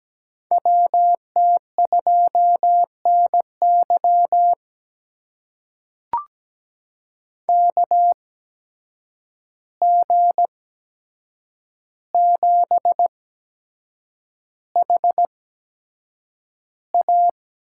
Morse code, heard as a keyed tone, about 17 wpm, spaced as for 7 wpm, 700 Hz. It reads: WT2NY KG7HA
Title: USA Call Signs - 5 Characters - 8x Character Spacing - 17wpm